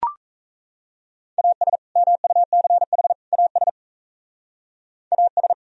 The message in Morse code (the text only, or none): AS MUCH AS AS